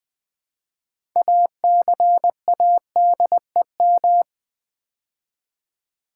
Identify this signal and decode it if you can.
Morse code, heard as a keyed tone, about 20 wpm, 700 Hz. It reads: ACADEM
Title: Build A Word - 2 Letters At A Time - 8-Letter Words - 20wpm